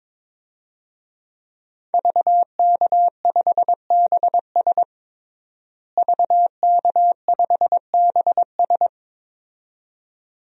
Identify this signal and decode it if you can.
Morse code, heard as a keyed tone, about 21 wpm, 700 Hz. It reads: VK5BS VK5BS
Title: DXCC-100 Call Signs - 5 Characters - Repeated 2x - 22wpm